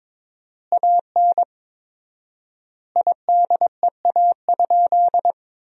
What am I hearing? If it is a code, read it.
Morse code, heard as a keyed tone, about 22 wpm, 700 Hz. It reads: AN IDEA?
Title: 4x Word-Spacing - Sentences from Top 400 Words - 22wpm